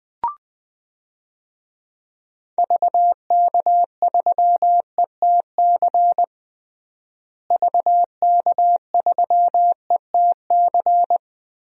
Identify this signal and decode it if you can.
Morse code, heard as a keyed tone, about 20 wpm, 700 Hz. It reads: VK3ETC VK3ETC
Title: DXCC-100 Call Signs - 6 Characters - Repeated 3x - 20wpm